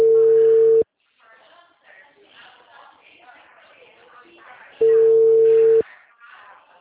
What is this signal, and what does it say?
Ringback tone (China, 450 Hz), with speech.